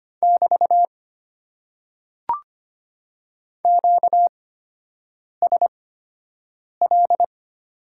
Morse code, heard as a keyed tone, about 25 wpm, spaced as for 7 wpm, 700 Hz. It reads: <BT> QSL
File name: Top 90 QSO Elements - 8x Character Spacing - 25wpm